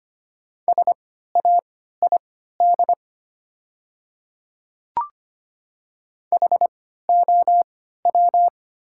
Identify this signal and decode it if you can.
Morse code, heard as a keyed tone, about 25 wpm, spaced as for 14 wpm, 700 Hz. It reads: SAID HOW
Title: 3x Character Spacing - Top 100 Words - 25wpm